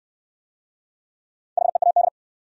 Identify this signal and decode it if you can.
Morse code, heard as a keyed tone, about 68 wpm, 700 Hz. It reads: HEAD